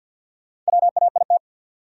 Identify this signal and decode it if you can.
Morse code, heard as a keyed tone, about 53 wpm, 700 Hz. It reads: WAIT
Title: Sentences from Top 500 Words - New Word Drill - 50wpm